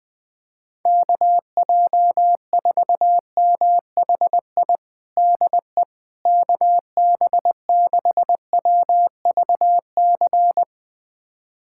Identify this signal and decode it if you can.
Morse code, heard as a keyed tone, about 21 wpm, 700 Hz. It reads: KJ4MHI DE KB6WVC
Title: USA Callsign DE Callsign - 6 Characters - 20wpm